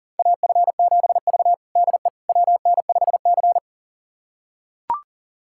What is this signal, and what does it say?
Morse code, heard as a keyed tone, about 40 wpm, 700 Hz. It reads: AF7V DE WN5C
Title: USA Callsign DE Callsign - 4 Characters - 40wpm